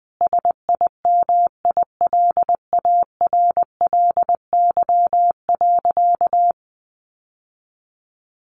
Morse code, heard as a keyed tone, about 20 wpm, 700 Hz. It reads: SIMILARLY.
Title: Sentences from Top 1400 Words 20wpm